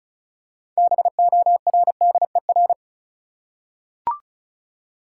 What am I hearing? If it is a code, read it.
Morse code, heard as a keyed tone, about 36 wpm, 700 Hz. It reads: BORDER
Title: Six Letter Words - 35wpm